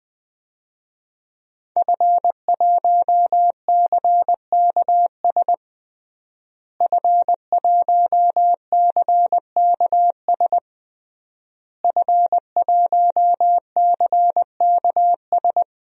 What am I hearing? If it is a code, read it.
Morse code, heard as a keyed tone, about 20 wpm, 700 Hz. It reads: F1CKS F1CKS F1CKS
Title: DXCC-100 Call Signs - 5 Characters - Repeated 3x - 20wpm